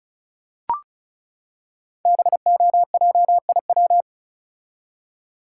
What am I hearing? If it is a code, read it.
Morse code, heard as a keyed tone, about 36 wpm, 700 Hz. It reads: BOJIW
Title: Code Groups - 5 Letters - 35wpm